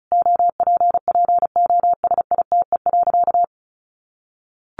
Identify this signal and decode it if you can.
Morse code, heard as a keyed tone, about 34 wpm, 700 Hz. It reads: OPPOSITE.
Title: Sentences from Top 1300 Words 35wpm